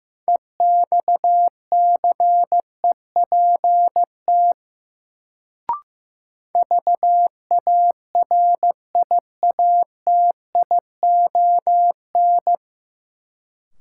Morse code, heard as a keyed tone, about 15 wpm, 700 Hz. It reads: EXCEPT VARIATION
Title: Sentences from Top 1600 Words - New Word Drill Alt - 15wpm